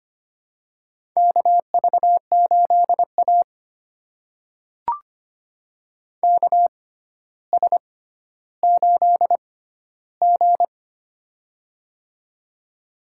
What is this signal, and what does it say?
Morse code, heard as a keyed tone, about 25 wpm, spaced as for 13 wpm, 700 Hz. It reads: KV8A KS8G
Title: USA Call Signs - 4 Characters - 6x Character Spacing - 25wpm